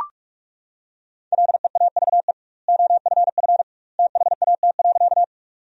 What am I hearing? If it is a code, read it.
Morse code, heard as a keyed tone, about 41 wpm, 700 Hz. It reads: LEAVE OFF THAT.
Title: 2x Word-Spacing - Sentences from Top 400 Words - 45wpm